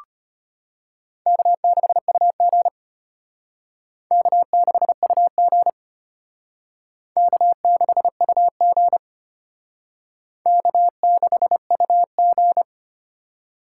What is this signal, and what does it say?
Morse code, heard as a keyed tone, about 32 wpm, 700 Hz. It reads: K6UG K6UG K6UG K6UG